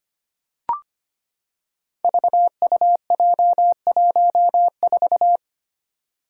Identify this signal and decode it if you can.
Morse code, heard as a keyed tone, about 25 wpm, 700 Hz. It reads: VUJ14
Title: Five Challenging Letter-Number Combinations 25wpm